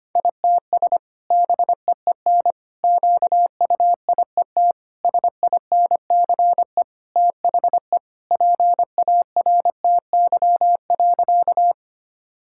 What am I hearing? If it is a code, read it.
Morse code, heard as a keyed tone, about 25 wpm, 700 Hz. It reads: ITS BEEN QUIET SINCE THE PARTY.
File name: practice / 25wpm / Sentences with 2-Word Phrases - 25wpm